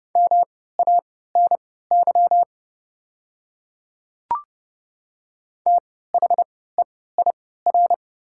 Morse code, heard as a keyed tone, about 30 wpm, spaced as for 17 wpm, 700 Hz. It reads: MANY THEIR
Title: 3x Character Spacing - Top 100 Words - 30wpm